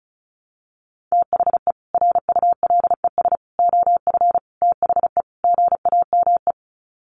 Morse code, heard as a keyed tone, about 34 wpm, 700 Hz. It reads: THE RULES OF THE GAME
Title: Sets of 5 Words Encore2 35wpm